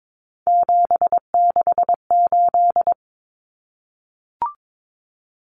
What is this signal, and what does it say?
Morse code, heard as a keyed tone, about 22 wpm, 700 Hz. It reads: 768
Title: Common Cut Numbers - 3-Digit 22wpm